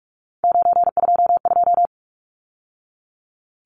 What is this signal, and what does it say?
Morse code, heard as a keyed tone, about 44 wpm, 700 Hz. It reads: 922
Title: Numbers - 3-Digit 45wpm